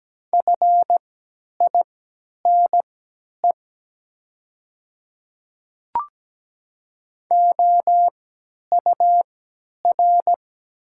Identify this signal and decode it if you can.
Morse code, heard as a keyed tone, about 17 wpm, spaced as for 10 wpm, 700 Hz. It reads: FINE OUR
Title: 3x Character Spacing - Top 500 Words - 17wpm